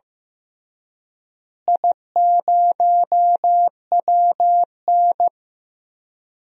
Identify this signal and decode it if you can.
Morse code, heard as a keyed tone, about 15 wpm, 700 Hz. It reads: I0WN
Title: DXCC-100 Call Signs - 4 Characters - Repeated 3x - 15wpm